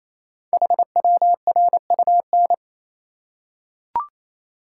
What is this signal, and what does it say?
Morse code, heard as a keyed tone, about 27 wpm, 700 Hz. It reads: HWRUN
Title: Code Groups - 5 Letters - 28wpm